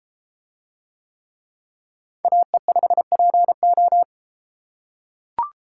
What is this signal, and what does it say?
Morse code, heard as a keyed tone, about 32 wpm, 700 Hz. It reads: AE5PO